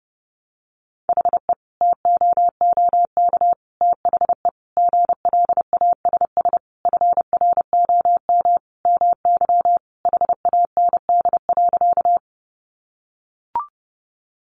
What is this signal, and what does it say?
Morse code, heard as a keyed tone, about 30 wpm, 700 Hz. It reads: HE TOOK THE GLASS FROM MY HAND.
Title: Sentences from Top 1300 Words 30wpm